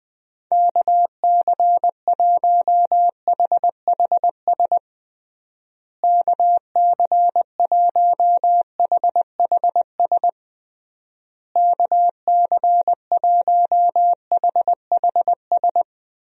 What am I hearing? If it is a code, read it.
Morse code, heard as a keyed tone, about 20 wpm, 700 Hz. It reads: KC1HHS KC1HHS KC1HHS
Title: USA Call Signs - 6 Characters - Repeated 3x - 20wpm